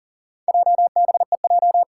Morse code, heard as a keyed tone, about 43 wpm, 700 Hz. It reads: JBEJ
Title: Four Challenging Letter-Number Combinations 40wpm